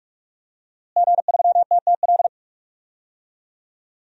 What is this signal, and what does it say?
Morse code, heard as a keyed tone, about 42 wpm, 700 Hz. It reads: G3TTL
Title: DXCC-100 Call Signs - 5 Characters - Repeated 3x - 45wpm